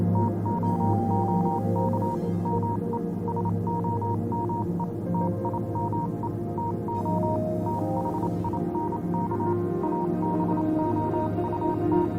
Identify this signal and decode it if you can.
Morse code, heard as a keyed tone, about 27 wpm, 940 Hz, with noise. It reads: TOOK ME SOME TIME TO THINK ABOUT
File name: Vylet Pony / aura / do you remember the song she sang (morse code version)